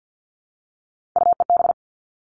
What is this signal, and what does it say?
Morse code, heard as a keyed tone, about 50 wpm, 700 Hz. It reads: UEB
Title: All Cut Numbers - 3-Digit 50wpm